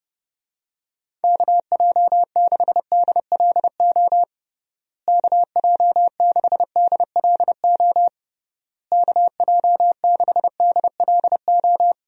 Morse code, heard as a keyed tone, about 30 wpm, 700 Hz. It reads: KJ6DLO KJ6DLO KJ6DLO
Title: USA Call Signs - 6 Characters - Repeated 3x - 30wpm